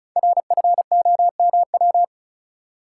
Morse code, heard as a keyed tone, about 34 wpm, 700 Hz. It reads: RFOMW
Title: Code Groups - 5 Letters - 35wpm